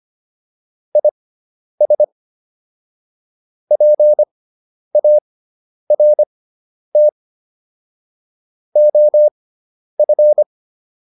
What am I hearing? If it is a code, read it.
Morse code, heard as a keyed tone, about 25 wpm, spaced as for 10 wpm, 600 Hz. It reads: IS PART OF